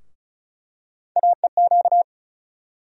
Morse code, heard as a keyed tone, about 34 wpm, 700 Hz. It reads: AEQ
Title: USA Call Sign Postfixes - 3 Letters 35wpm